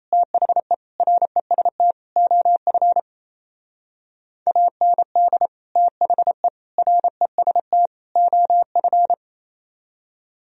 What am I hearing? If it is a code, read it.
Morse code, heard as a keyed tone, about 36 wpm, 700 Hz. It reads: THE REST OF AND THE REST OF